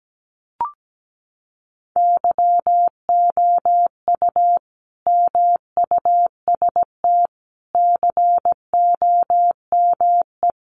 Morse code, heard as a keyed tone, about 17 wpm, 700 Hz. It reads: YOU MUST COME
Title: Sentences from Top 200 Words 17wpm